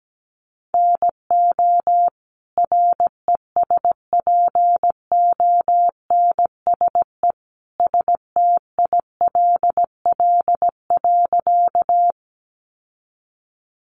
Morse code, heard as a keyed tone, about 17 wpm, 700 Hz. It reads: NO RESPONSE STILL.
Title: Sentences from Top 1400 Words 17wpm